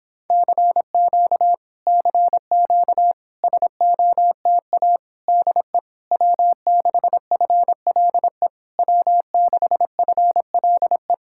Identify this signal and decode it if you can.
Morse code, heard as a keyed tone, about 26 wpm, 700 Hz. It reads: CQ CQ SOTA DE W6FLE W6FLE